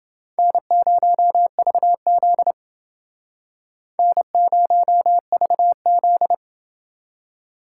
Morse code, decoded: N0VZ N0VZ

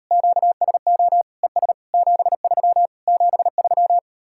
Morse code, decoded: QSO ES 73 73

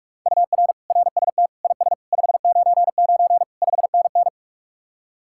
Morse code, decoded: UR RST IS 599 5NN

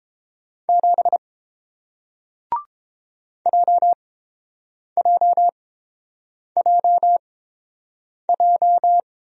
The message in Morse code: 7 JJJJ

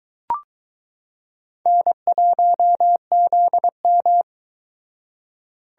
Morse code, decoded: N1ZM

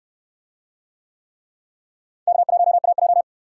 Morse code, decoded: K1AJ

